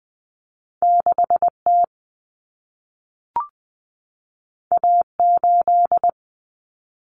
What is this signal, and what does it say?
Morse code, heard as a keyed tone, about 20 wpm, 700 Hz. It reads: 6T A8
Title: Common Cut Numbers - 2-Digit 20wpm